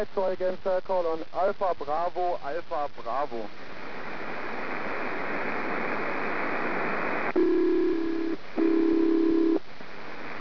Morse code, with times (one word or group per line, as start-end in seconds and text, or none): none